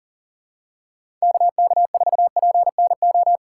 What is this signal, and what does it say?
Morse code, heard as a keyed tone, about 38 wpm, 700 Hz. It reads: KK4PNO
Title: USA Call Signs - 6 Characters - 4x Character Spacing - 40wpm